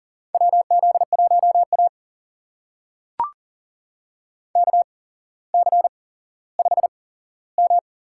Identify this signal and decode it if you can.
Morse code, heard as a keyed tone, about 42 wpm, spaced as for 17 wpm, 700 Hz. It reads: WZ1A KC5M
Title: USA Call Signs - 4 Characters - 8x Character Spacing - 40wpm